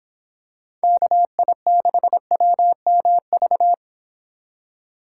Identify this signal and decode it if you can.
Morse code, heard as a keyed tone, about 26 wpm, 700 Hz. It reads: KI6WMV